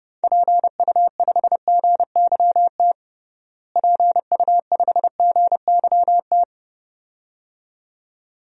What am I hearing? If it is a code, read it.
Morse code, heard as a keyed tone, about 30 wpm, 700 Hz. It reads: PU5GYT PU5GYT
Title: DXCC-100 Call Signs - 6 Characters - Repeated 2x - 30wpm